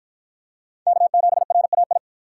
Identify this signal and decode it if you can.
Morse code, heard as a keyed tone, about 48 wpm, 700 Hz. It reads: K7RAI